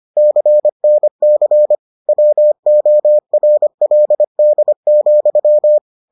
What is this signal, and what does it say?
Morse code, heard as a keyed tone, about 25 wpm, 600 Hz. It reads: CNC WORLD,